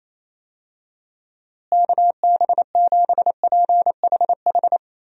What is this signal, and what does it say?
Morse code, heard as a keyed tone, about 28 wpm, 700 Hz. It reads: KB7PHH
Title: USA Call Signs - 6 Characters - Repeated 3x - 28wpm